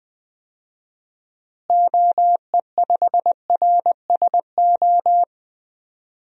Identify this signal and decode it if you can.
Morse code, heard as a keyed tone, about 20 wpm, 700 Hz. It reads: OE5RSO
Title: DXCC-100 Call Signs - 6 Characters - Repeated 2x - 20wpm